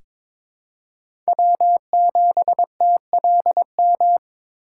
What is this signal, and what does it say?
Morse code, heard as a keyed tone, about 22 wpm, 700 Hz. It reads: W7TLM
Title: USA Call Signs - 5 Characters - 4x Character Spacing - 22wpm